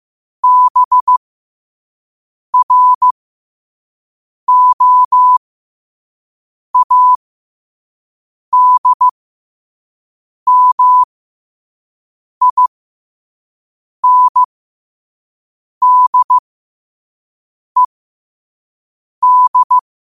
Morse code, heard as a keyed tone, about 15 wpm, spaced as for 5 wpm, 1000 Hz. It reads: BROADMINDED
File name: qotdfiles / 5wpm